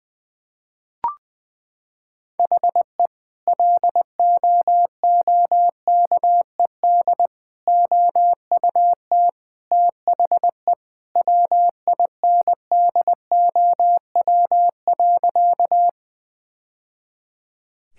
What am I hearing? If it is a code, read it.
Morse code, heard as a keyed tone, about 20 wpm, 700 Hz. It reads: HE LOOKED OUT THE WINDOW.
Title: Sentences with 2-Word Phrases - 20wpm